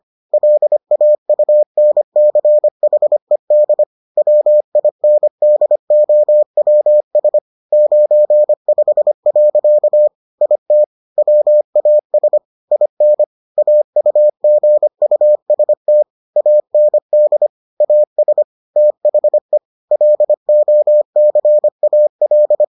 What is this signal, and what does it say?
Morse code, heard as a keyed tone, about 25 wpm, 600 Hz. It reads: LAUNCHED WINDOWS 95. IT WAS IN AUGUST AND AS THE LOCAL